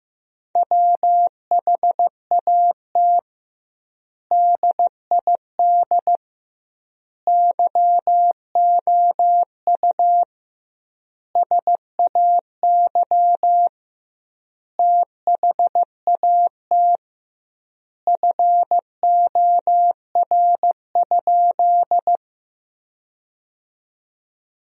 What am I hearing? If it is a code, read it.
Morse code, heard as a keyed tone, about 15 wpm, 700 Hz. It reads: WHAT DID YOU SAY THAT FOR?
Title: 2x Word-Spacing - Sentences from Top 200 Words - 15wpm